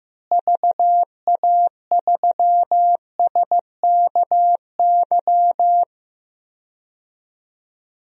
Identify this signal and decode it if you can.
Morse code, heard as a keyed tone, about 15 wpm, 700 Hz. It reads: VA3SKY
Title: DXCC-100 Call Signs - 6 Characters - Repeated 3x - 15wpm